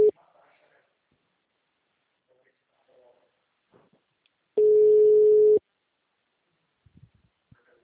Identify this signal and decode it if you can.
Ringback tone (Europe, 425 Hz), with speech.